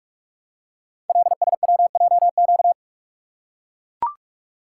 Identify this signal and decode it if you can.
Morse code, heard as a keyed tone, about 44 wpm, 700 Hz. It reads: LSWJQ
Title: Code Groups - 5 Letter:Numbers - 45wpm